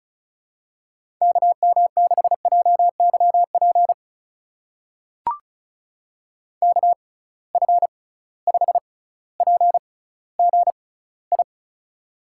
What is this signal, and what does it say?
Morse code, heard as a keyed tone, about 34 wpm, spaced as for 18 wpm, 700 Hz. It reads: KM6JYP KF5PGI
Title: USA Call Signs - 6 Characters - 6x Character Spacing - 35wpm